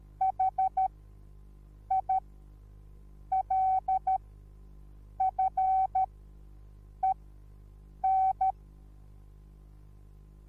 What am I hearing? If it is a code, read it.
Morse code, heard as a keyed tone, about 13 wpm, spaced as for 6 wpm, 760 Hz. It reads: HILFEN